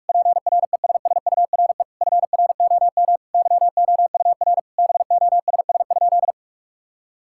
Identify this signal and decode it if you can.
Morse code, heard as a keyed tone, about 46 wpm, 700 Hz. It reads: PRESSURE FROM YOUR BOSS?
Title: Sentences from Top 1000 Words - Review Part 6 - 45wpm